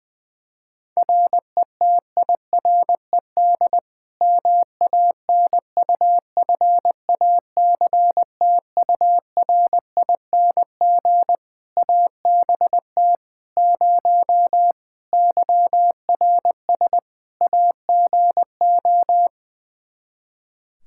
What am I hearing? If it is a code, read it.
Morse code, heard as a keyed tone, about 21 wpm, 700 Hz. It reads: RETIRED MANUFACTURING ABT 0 YRS AGO
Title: Practice QSO - Exchange 3 - More Info - 20wpm